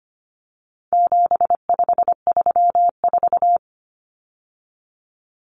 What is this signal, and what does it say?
Morse code, heard as a keyed tone, about 25 wpm, 700 Hz. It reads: 7534
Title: Common Cut Numbers - 4-Digit 25wpm